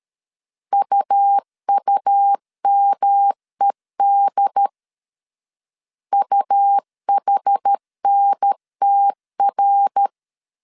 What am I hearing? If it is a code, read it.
Morse code, heard as a keyed tone, about 13 wpm, 800 Hz. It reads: UUMED UHNTR
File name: CW / DET / 1 3 1500C S C 12wpm 2750